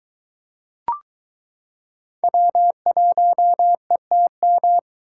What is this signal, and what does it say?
Morse code, heard as a keyed tone, about 22 wpm, 700 Hz. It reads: W1ETM